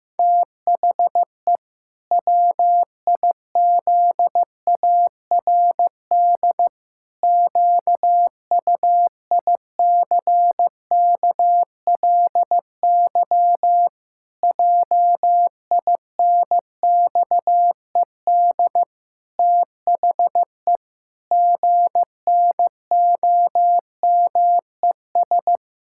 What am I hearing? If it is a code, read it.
Morse code, heard as a keyed tone, about 16 wpm, 700 Hz. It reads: THE WIZARD QUICKLY JINXED THE GNOMES
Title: Pangram Challenge v2 - 15wpm